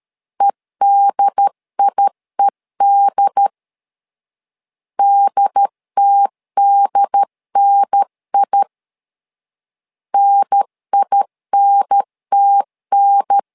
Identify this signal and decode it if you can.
Morse code, heard as a keyed tone, about 13 wpm, 800 Hz. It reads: EDIED DTDNI NINTN
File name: LEZIONI / 2 UHDN / 2 UHDN 12wpm 1456